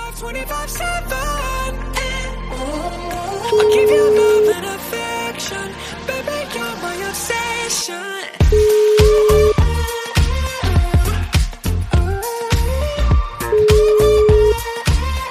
Ringback tone (Europe, 425 Hz), with noise.